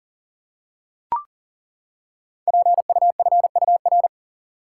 Morse code, heard as a keyed tone, about 42 wpm, 700 Hz. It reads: PUFUR
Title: Code Groups - 5 Letters - 40wpm